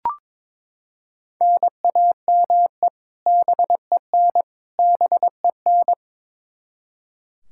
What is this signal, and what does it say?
Morse code, heard as a keyed tone, about 21 wpm, 700 Hz. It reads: NAME BEN BEN
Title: K1USN SST QSO Practice 22wpm